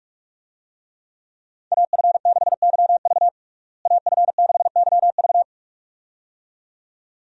Morse code, heard as a keyed tone, about 46 wpm, 700 Hz. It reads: AF6YV AF6YV
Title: USA Call Signs - 5 Characters - Repeated 2x - 45wpm